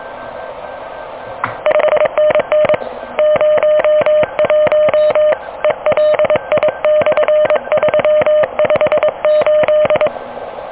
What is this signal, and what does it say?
Morse code, heard as a keyed tone, about 21 wpm, 610 Hz. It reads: 5NN 01 ELI/358
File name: OK QRP 2023 / OK QRP 2023